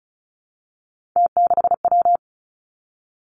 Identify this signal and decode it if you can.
Morse code, heard as a keyed tone, about 36 wpm, 700 Hz. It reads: T6W
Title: Three Challenging Letter-Number Combinations 35wpm